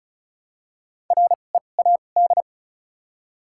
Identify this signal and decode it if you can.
Morse code, heard as a keyed tone, about 36 wpm, spaced as for 25 wpm, 700 Hz. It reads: READ
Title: 2x Character Spacing - Top 200 Words - 35wpm